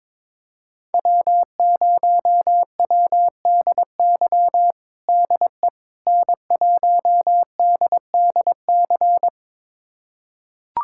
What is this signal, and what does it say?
Morse code, heard as a keyed tone, about 22 wpm, 700 Hz. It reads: W0WDY DE N1DDC
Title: USA Callsign DE Callsign - 5 Characters - 22wpm